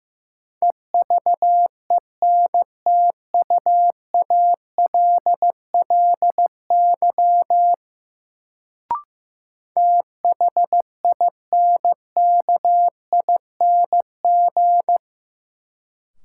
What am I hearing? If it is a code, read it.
Morse code, heard as a keyed tone, about 15 wpm, 700 Hz. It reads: EVENTUALLY THINKING
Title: Sentences from Top 1000 Words - Review Part 6 - New Word Drill - 15wpm